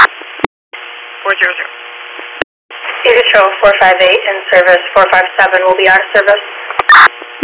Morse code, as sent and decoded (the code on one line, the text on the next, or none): none